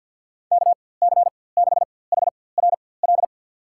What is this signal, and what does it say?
Morse code, heard as a keyed tone, about 52 wpm, spaced as for 23 wpm, 700 Hz. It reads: KC6HRL
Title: USA Call Signs - 4x Character Spacing - 50wpm